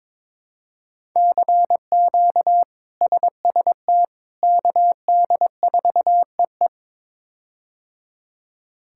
Morse code, heard as a keyed tone, about 22 wpm, 700 Hz. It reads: CQ SST KD4EE
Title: K1USN SST QSO Practice - 22wpm